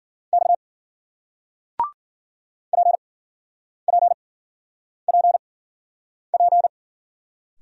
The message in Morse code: X P P P P

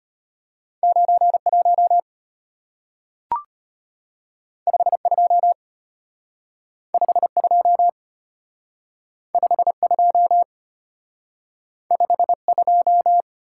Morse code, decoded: 91 52 52 52 52

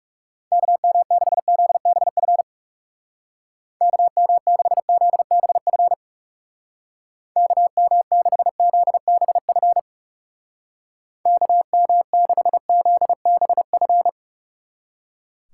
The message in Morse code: KM6ZBF KM6ZBF KM6ZBF KM6ZBF

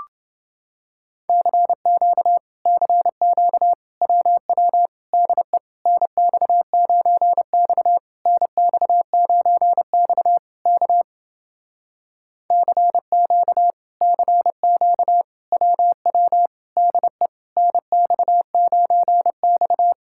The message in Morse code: CQ CQ WW DE NX9X NX9X K CQ CQ WW DE NX9X

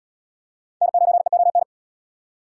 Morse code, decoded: N9EON

